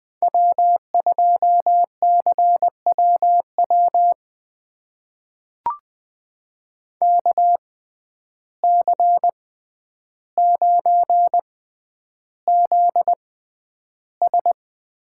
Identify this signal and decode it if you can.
Morse code, heard as a keyed tone, about 20 wpm, spaced as for 10 wpm, 700 Hz. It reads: W2CWW KC9ZS
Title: USA Call Signs - 6x Character Spacing - 20wpm